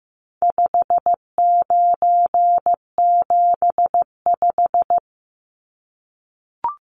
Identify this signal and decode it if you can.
Morse code, heard as a keyed tone, about 15 wpm, 700 Hz. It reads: 5975
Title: Numbers - 4-Digit 15wpm